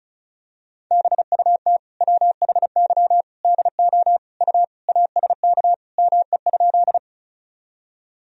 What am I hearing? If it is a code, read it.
Morse code, heard as a keyed tone, about 36 wpm, 700 Hz. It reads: BUT WHY DO U ASK ME?
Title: Sentences from Top 200 Words and Abbreviations 35wpm